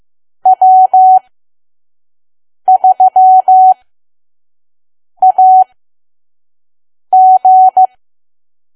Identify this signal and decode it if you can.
Morse code, heard as a keyed tone, about 15 wpm, spaced as for 5 wpm, 750 Hz. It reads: W3AG
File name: Archive / 5 WPM files / PM